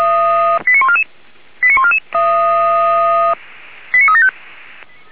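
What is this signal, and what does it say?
Five-tone selective call (ZVEI), heard as quick runs of five tones, tones of about 70 ms. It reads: 89144 89144 89375
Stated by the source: Five-tone ZVEI selective call, received via SDR.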